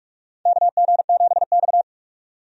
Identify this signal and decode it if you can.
Morse code, heard as a keyed tone, about 45 wpm, 700 Hz. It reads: KG7X
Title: USA Call Signs - 4 Characters - 45wpm